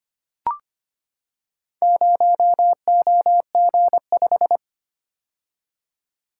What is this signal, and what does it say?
Morse code, heard as a keyed tone, about 25 wpm, 700 Hz. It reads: 0OG5
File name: practice / 25wpm / Four Letter-Number Combinations - 25wpm